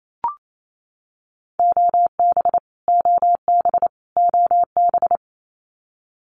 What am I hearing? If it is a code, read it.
Morse code, heard as a keyed tone, about 28 wpm, 700 Hz. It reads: OB OB OB
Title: Abbreviations Repeated 3x - 28wpm